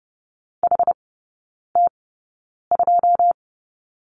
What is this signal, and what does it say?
Morse code, heard as a keyed tone, about 30 wpm, spaced as for 9 wpm, 690 Hz. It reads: HT2
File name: Lesson 17e - Letter u - Three Characters 30wpm